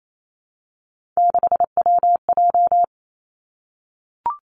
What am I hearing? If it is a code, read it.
Morse code, heard as a keyed tone, about 28 wpm, 700 Hz. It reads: 6WJ
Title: Three Challenging Letter-Number Combinations 28wpm